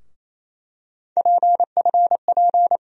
Morse code, heard as a keyed tone, about 27 wpm, 700 Hz. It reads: PFP
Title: USA Call Sign Postfixes - 3 Letters 28wpm